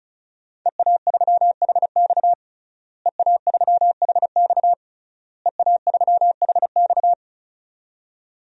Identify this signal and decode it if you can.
Morse code, heard as a keyed tone, about 34 wpm, 700 Hz. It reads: EA3HX EA3HX EA3HX
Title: DXCC-100 Call Signs - 5 Characters - Repeated 3x - 35wpm